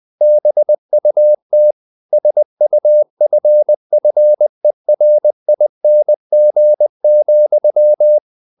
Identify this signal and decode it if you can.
Morse code, heard as a keyed tone, about 20 wpm, 600 Hz. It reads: BUT SUFFERING,
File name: the old man and the sea / the old man and the sea part 1 20wpm